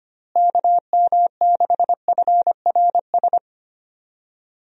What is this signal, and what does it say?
Morse code, heard as a keyed tone, about 25 wpm, 700 Hz. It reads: KM6FRS
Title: USA Call Signs - 6 Characters - Repeated 3x - 25wpm